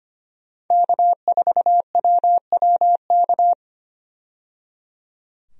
Morse code, heard as a keyed tone, about 25 wpm, 700 Hz. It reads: K4WWK
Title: USA Call Signs - 5 Characters - 25wpm